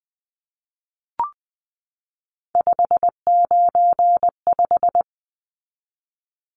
Morse code, heard as a keyed tone, about 20 wpm, 700 Hz. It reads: 595
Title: Numbers - 3-Digit 20wpm